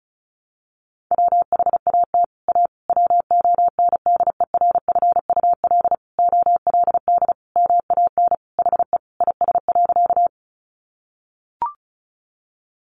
Morse code, heard as a keyed tone, about 34 wpm, 700 Hz. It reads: WHAT A WONDERFUL OLD MAN HE IS.
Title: Sentences from Top 1100 Words 35wpm